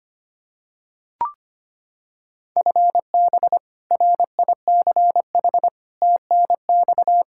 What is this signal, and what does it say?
Morse code, heard as a keyed tone, about 25 wpm, 700 Hz. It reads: FB RICH TNX
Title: Practice QSO - Exchange 3 - More Info - 25wpm